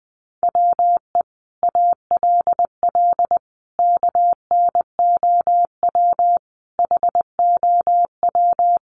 Morse code, heard as a keyed tone, about 20 wpm, 700 Hz. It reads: WE ALL KNOW HOW
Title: Sentences from Top 1200 Words 20wpm